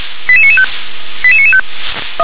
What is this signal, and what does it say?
Five-tone selective call (ZVEI), heard as quick runs of five tones, tones of about 70 ms. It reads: 88995 88995